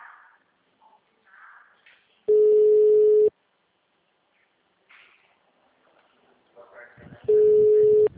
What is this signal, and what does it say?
Ringback tone (Europe, 425 Hz).